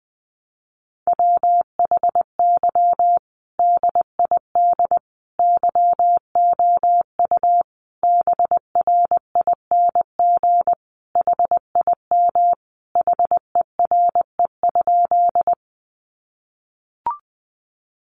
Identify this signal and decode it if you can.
Morse code, heard as a keyed tone, about 20 wpm, 700 Hz. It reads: WHY DID YOU BRING HIM HERE?
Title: Sentences from Top 500 Words 20wpm